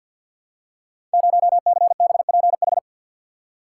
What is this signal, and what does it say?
Morse code, heard as a keyed tone, about 52 wpm, 700 Hz. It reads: 0CBPH
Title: Code Groups - 5 Letter:Numbers - 50wpm